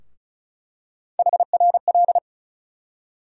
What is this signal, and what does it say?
Morse code, heard as a keyed tone, about 36 wpm, 700 Hz. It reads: HRL
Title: USA Call Sign Postfixes - 3 Letters 35wpm